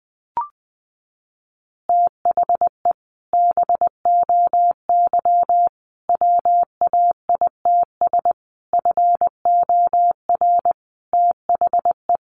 Morse code, heard as a keyed tone, about 20 wpm, 700 Hz. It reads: THE BOY WAITS FOR THE
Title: Additional Easy Phrases 20wpm